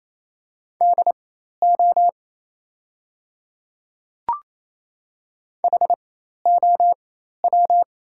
Morse code, heard as a keyed tone, about 27 wpm, spaced as for 13 wpm, 690 Hz. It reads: DO HOW
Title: 4x Character Spacing - Top 100 Words - 28wpm